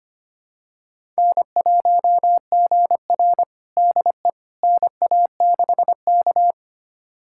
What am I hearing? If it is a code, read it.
Morse code, heard as a keyed tone, about 25 wpm, 700 Hz. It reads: N1GR DE NA6K
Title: USA Callsign DE Callsign - 4 Characters - 25wpm